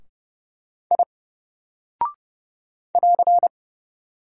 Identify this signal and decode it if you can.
Morse code, heard as a keyed tone, about 30 wpm, 700 Hz. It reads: I <AR>
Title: Single Character (Letter:Number:Punctuation:Prosign) 30wpm